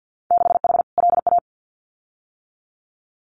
Morse code, heard as a keyed tone, about 50 wpm, 700 Hz. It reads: 6H LA
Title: Winter Field Day Exchanges Encore 50wpm